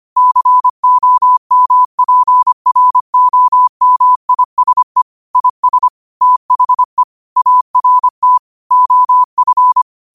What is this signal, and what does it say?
Morse code, heard as a keyed tone, about 25 wpm, 1000 Hz. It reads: COMPROMISE IS THE ART OF
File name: qotdfiles / 25wpm